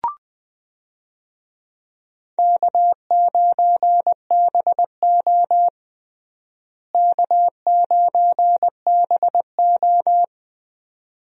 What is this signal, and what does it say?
Morse code, heard as a keyed tone, about 20 wpm, 700 Hz. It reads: K9BO K9BO
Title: USA Call Signs - 4 Characters - Repeated 2x - 20wpm